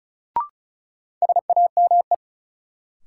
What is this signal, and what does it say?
Morse code, heard as a keyed tone, about 36 wpm, 700 Hz. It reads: SAME
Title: Sets of 2 Words - New Word Drill - 35wpm